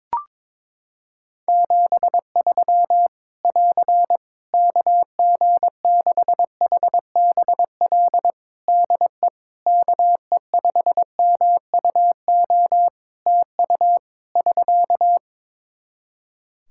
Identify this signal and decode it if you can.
Morse code, heard as a keyed tone, about 22 wpm, 700 Hz. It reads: 73 <AR> KG6HBL DE KE5MUO TU <SK>
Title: Practice QSO - Exchange 4 - Ending - 22wpm